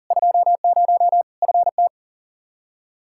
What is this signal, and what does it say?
Morse code, heard as a keyed tone, about 40 wpm, 700 Hz. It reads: 20 FT
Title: Practice QSO - Exchange 2 - Info - 40wpm